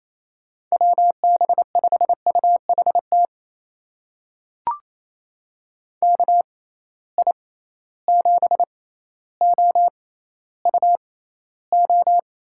Morse code, heard as a keyed tone, about 28 wpm, spaced as for 14 wpm, 700 Hz. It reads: WB5UHT KI7OUO
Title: USA Call Signs - 6x Character Spacing - 28wpm